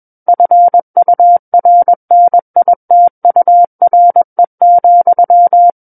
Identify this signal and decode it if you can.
Morse code, heard as a keyed tone, about 21 wpm, 700 Hz. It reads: FURNITURE,
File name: Book One / 12-Chapter 12 - 21 WPM